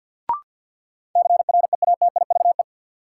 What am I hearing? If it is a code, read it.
Morse code, heard as a keyed tone, about 51 wpm, 700 Hz. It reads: CREATIVE
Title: Sentences from Top 1000 Words - Review Part 5 - New Word Drill - 50wpm